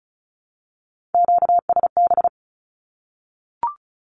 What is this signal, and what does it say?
Morse code, heard as a keyed tone, about 34 wpm, 700 Hz. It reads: QSB
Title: Q-Codes Repeated 3x 35wpm